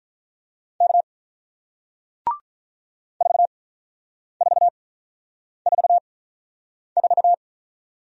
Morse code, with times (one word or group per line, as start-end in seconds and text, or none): 0.8-1.0 K
3.2-7.3 4444